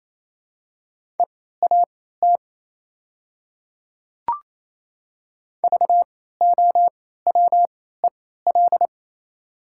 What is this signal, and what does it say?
Morse code, heard as a keyed tone, about 27 wpm, spaced as for 16 wpm, 690 Hz. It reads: EAT VOWEL
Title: 3x Character Spacing - Top 500 Words - 28wpm